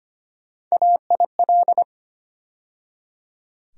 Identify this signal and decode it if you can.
Morse code, heard as a keyed tone, about 25 wpm, 700 Hz. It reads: AIL
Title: USA Call Sign Postfixes - 3 Letters 25wpm